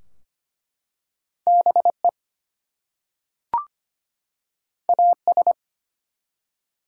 Morse code, heard as a keyed tone, about 25 wpm, 700 Hz. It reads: BE AS
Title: Two Letter Words - 25wpm